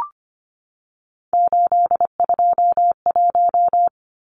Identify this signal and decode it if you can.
Morse code, heard as a keyed tone, about 25 wpm, 700 Hz. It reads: 821
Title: Numbers - 3-Digit 25wpm